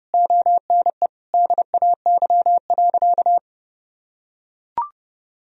Morse code, heard as a keyed tone, about 30 wpm, 700 Hz. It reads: ONE DAY.
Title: Sentences from Top 500 Words and Abbreviations 30wpm